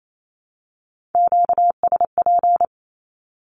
Morse code, heard as a keyed tone, about 28 wpm, 700 Hz. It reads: QSP
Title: Q-Codes Repeated 3x 28wpm